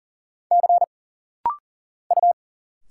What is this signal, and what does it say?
Morse code, heard as a keyed tone, about 41 wpm, 700 Hz. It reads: C U
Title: Single Letters - Rapid-Fire 40wpm